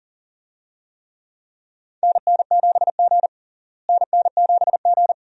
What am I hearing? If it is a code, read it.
Morse code, heard as a keyed tone, about 40 wpm, 700 Hz. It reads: NN7G NN7G
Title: USA Call Signs - 4 Characters - Repeated 2x - 40wpm